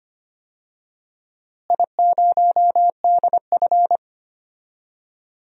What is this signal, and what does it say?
Morse code, heard as a keyed tone, about 25 wpm, 700 Hz. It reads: I0DF
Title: DXCC-100 Call Signs - 4 Characters - Repeated 3x - 25wpm